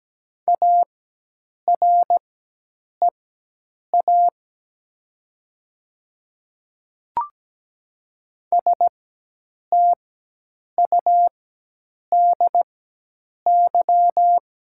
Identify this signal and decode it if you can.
Morse code, heard as a keyed tone, about 17 wpm, spaced as for 8 wpm, 700 Hz. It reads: AREA STUDY
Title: 4x Character Spacing - Top 500 Words - 17wpm